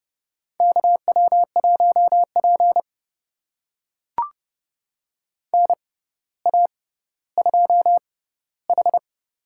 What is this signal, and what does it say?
Morse code, heard as a keyed tone, about 30 wpm, spaced as for 15 wpm, 700 Hz. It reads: KW1P NA2H
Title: USA Call Signs - 6x Character Spacing - 30wpm